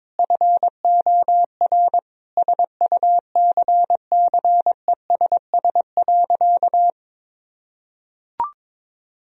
Morse code, heard as a keyed tone, about 22 wpm, 700 Hz. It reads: FOR SUCCESS.
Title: Sentences from Top 1000 Words - Review Part 4 - 22wpm